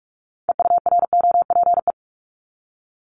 Morse code, heard as a keyed tone, about 46 wpm, 700 Hz. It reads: EUROPE
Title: Multi-letter Phonemes Repeated 3x 45wpm